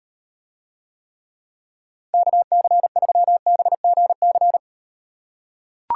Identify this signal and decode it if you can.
Morse code, heard as a keyed tone, about 38 wpm, 700 Hz. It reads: KC3BGC